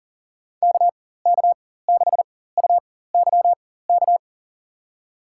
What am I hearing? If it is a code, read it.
Morse code, heard as a keyed tone, about 40 wpm, spaced as for 19 wpm, 700 Hz. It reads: KK6UYK
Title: USA Call Signs - 6 Characters - 4x Character Spacing - 40wpm